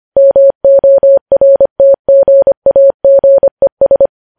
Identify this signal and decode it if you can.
Morse code, heard as a keyed tone, about 25 wpm, 550 Hz. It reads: MORTGAGES
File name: podcast / News headlines-1x-25x25